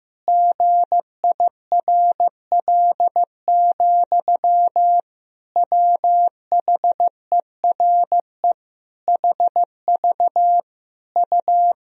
Morse code, heard as a keyed tone, about 15 wpm, 700 Hz. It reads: GIRL, WHERE HV U
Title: Sentences from Top 400 Words and Abbreviations 15wpm